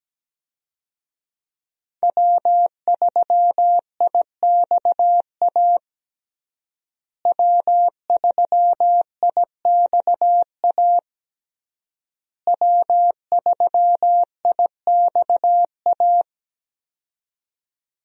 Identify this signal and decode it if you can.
Morse code, heard as a keyed tone, about 17 wpm, 700 Hz. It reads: W3IXA W3IXA W3IXA
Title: USA Call Signs - 5 Characters - Repeated 3x - 17wpm